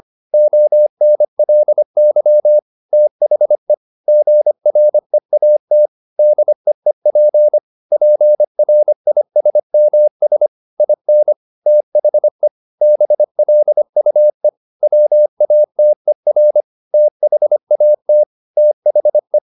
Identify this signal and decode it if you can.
Morse code, heard as a keyed tone, about 25 wpm, 600 Hz. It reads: ONLY THE GREAT DEEP PRISMS IN THE BLUE WATER THAT THE
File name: the old man and the sea part 1 25wpm